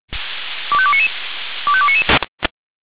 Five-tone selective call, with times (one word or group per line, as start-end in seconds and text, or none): 0.7-1.1 25299
1.7-2.0 25299